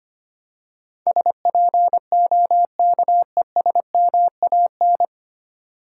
Morse code, heard as a keyed tone, about 25 wpm, 700 Hz. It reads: SPOKESMAN
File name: Nine Letter Words - 25wpm